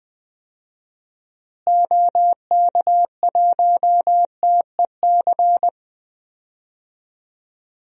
Morse code, heard as a keyed tone, about 20 wpm, 700 Hz. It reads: OK1TEC
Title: DXCC-100 Call Signs - 6 Characters - Repeated 3x - 20wpm